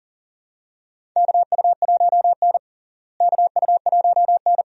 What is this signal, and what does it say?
Morse code, heard as a keyed tone, about 40 wpm, 700 Hz. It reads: KU1N KU1N
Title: USA Call Signs - 4 Characters - Repeated 2x - 40wpm